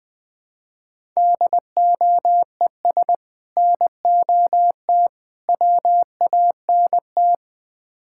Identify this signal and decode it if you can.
Morse code, heard as a keyed tone, about 20 wpm, 700 Hz. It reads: DOES NOT WANT
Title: Build A Phrase - 4-Word Phrases - 20wpm